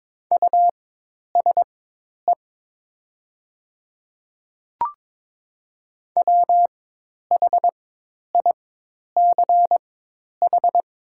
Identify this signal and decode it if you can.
Morse code, heard as a keyed tone, about 22 wpm, spaced as for 10 wpm, 700 Hz. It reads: USE WHICH
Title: 4x Character Spacing - Top 100 Words - 22wpm